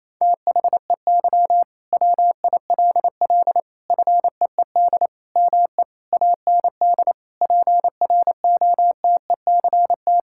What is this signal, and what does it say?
Morse code, heard as a keyed tone, about 28 wpm, 700 Hz. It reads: THEY WILL FEED ME AND PROTECT
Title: Sentences from Top 1600 Words 28wpm